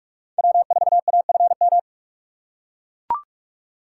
Morse code, heard as a keyed tone, about 43 wpm, 700 Hz. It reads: W4AFM
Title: USA Call Signs - 5 Characters - 4x Character Spacing - 45wpm